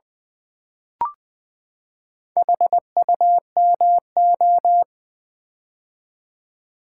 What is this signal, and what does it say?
Morse code, heard as a keyed tone, about 20 wpm, 700 Hz. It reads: HUMO
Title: Build A Word - 4 Letters At A Time - 8-Letter Words - 20wpm